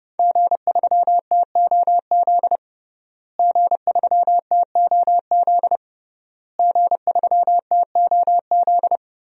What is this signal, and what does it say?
Morse code, heard as a keyed tone, about 30 wpm, 700 Hz. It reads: G3TOZ G3TOZ G3TOZ
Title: DXCC-100 Call Signs - 5 Characters - Repeated 3x - 30wpm